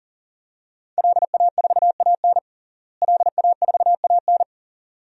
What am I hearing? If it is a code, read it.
Morse code, heard as a keyed tone, about 37 wpm, 700 Hz. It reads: LA4AN LA4AN
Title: DXCC-100 Call Signs - 5 Characters - Repeated 2x - 40wpm